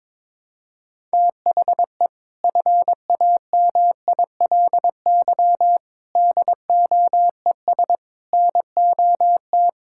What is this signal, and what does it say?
Morse code, heard as a keyed tone, about 22 wpm, 700 Hz. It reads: THE FAMILY DOES NOT
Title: Sentences from Top 1000 Words - Review Part 5 - 22wpm